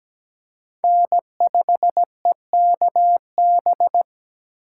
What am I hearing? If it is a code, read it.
Morse code, heard as a keyed tone, about 20 wpm, 700 Hz. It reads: N5EKB
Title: Speed Racing - USA 1x3 Call Signs 17wpm